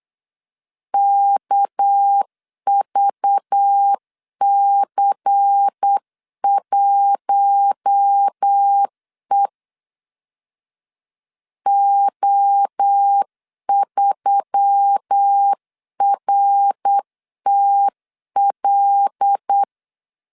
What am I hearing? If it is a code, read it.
Morse code, heard as a keyed tone, about 8 wpm, 800 Hz. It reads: KVC1E O3RTL
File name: CW / DET / 1 6 1500C S C 8wpm 5219